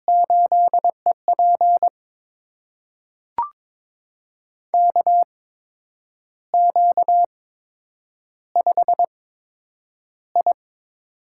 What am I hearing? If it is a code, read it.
Morse code, heard as a keyed tone, about 22 wpm, spaced as for 8 wpm, 700 Hz. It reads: 8EP KQ5I
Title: USA Call Signs - 4 Characters - 8x Character Spacing - 22wpm